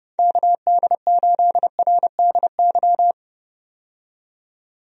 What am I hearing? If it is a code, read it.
Morse code, heard as a keyed tone, about 30 wpm, 700 Hz. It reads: KD8RDY
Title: USA Call Signs - 6 Characters - 30wpm